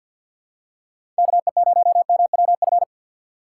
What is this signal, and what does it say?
Morse code, heard as a keyed tone, about 52 wpm, 700 Hz. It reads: KE0MWF
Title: USA Call Signs - 6 Characters - 4x Character Spacing - 50wpm